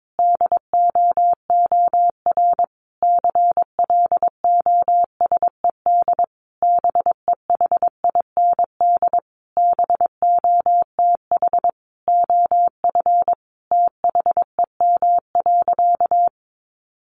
Morse code, heard as a keyed tone, about 22 wpm, 700 Hz. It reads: DOOR CLOSED BEHIND BOTH OF THEM.
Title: Sentences from Top 1400 Words 22wpm